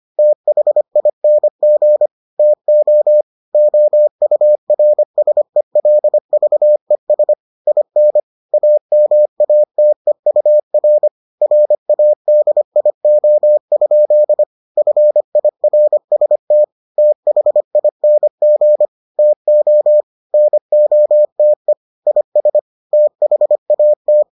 Morse code, heard as a keyed tone, about 25 wpm, 600 Hz. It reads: THING TO OURSELVES IN AMATEUR RADIO? FIRST THING TO NOTE IS THAT